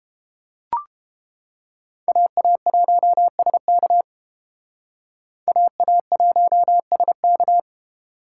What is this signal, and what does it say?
Morse code, heard as a keyed tone, about 29 wpm, 700 Hz. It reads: AA1SK AA1SK